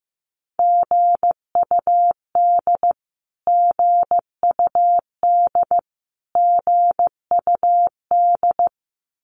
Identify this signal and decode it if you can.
Morse code, heard as a keyed tone, about 15 wpm, 700 Hz. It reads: GUD GUD GUD
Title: Abbreviations Repeated 3x - 15wpm